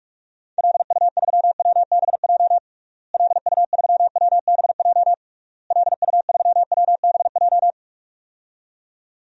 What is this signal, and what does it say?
Morse code, heard as a keyed tone, about 46 wpm, 700 Hz. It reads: LU3WBJ LU3WBJ LU3WBJ
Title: DXCC-100 Call Signs - 6 Characters - Repeated 3x - 45wpm